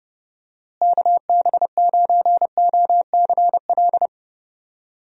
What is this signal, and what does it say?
Morse code, heard as a keyed tone, about 32 wpm, 700 Hz. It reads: KB9OCL
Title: USA Call Signs - 6 Characters - Repeated 3x - 30wpm